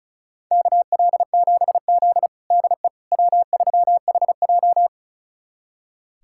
Morse code, heard as a keyed tone, about 36 wpm, 700 Hz. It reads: KL7Z DE W3HJ
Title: USA Callsign DE Callsign - 4 Characters - 35wpm